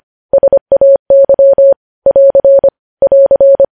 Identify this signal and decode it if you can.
Morse code, heard as a keyed tone, about 25 wpm, 550 Hz. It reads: SAY <AR> <AR>
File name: podcast / News headlines-1x-25x25